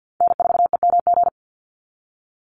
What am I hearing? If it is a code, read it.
Morse code, heard as a keyed tone, about 50 wpm, 700 Hz. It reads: N4EMG
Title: NA SOTA Chaser Callsigns 50wpm